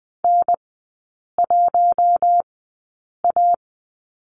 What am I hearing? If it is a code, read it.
Morse code, heard as a keyed tone, about 20 wpm, spaced as for 8 wpm, 700 Hz. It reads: N1A
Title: 041 Lesson 8e - Number 1 - Three Characters 20wpm